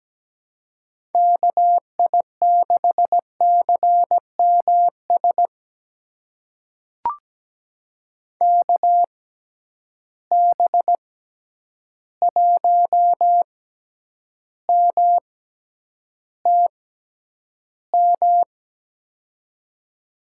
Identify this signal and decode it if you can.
Morse code, heard as a keyed tone, about 17 wpm, spaced as for 9 wpm, 700 Hz. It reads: KI6CMS KB1MTM
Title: USA Call Signs - 6 Characters - 6x Character Spacing - 17wpm